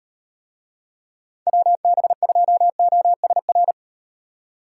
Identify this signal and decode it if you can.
Morse code, heard as a keyed tone, about 36 wpm, 700 Hz. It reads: WB2OSR